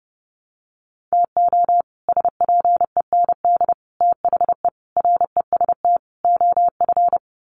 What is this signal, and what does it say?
Morse code, heard as a keyed tone, about 30 wpm, 700 Hz. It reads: TO SPEND THE REST OF
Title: Sets of 5 Words Encore2 30wpm